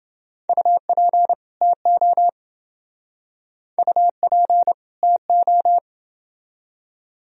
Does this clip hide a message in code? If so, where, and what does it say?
Yes: Morse code from 0.5 to 5.8 s, heard as a keyed tone, about 28 wpm, 700 Hz. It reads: UP TO UP TO